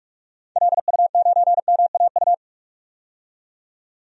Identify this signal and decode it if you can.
Morse code, heard as a keyed tone, about 46 wpm, 700 Hz. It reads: LU9MAU
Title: DXCC-100 Call Signs - 6 Characters - 45wpm